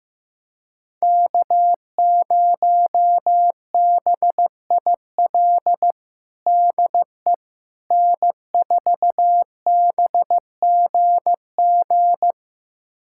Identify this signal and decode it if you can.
Morse code, heard as a keyed tone, about 15 wpm, 700 Hz. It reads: K0BIL DE N4BGG
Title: USA Callsign DE Callsign - 5 Characters - 15wpm